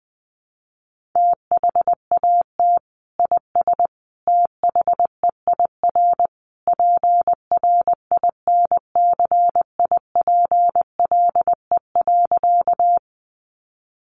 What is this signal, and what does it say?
Morse code, heard as a keyed tone, about 19 wpm, 700 Hz. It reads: THAT IS THEIR PRINCIPLE.
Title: Sentences from Top 1400 Words 20wpm